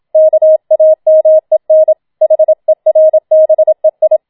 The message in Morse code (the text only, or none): KAMEN HERBEI